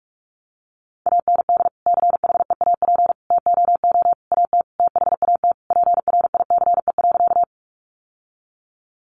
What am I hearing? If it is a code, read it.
Morse code, heard as a keyed tone, about 44 wpm, 700 Hz. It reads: AND CHEAP TOO AT THAT PRICE.
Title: Sentences from Top 1200 Words 45wpm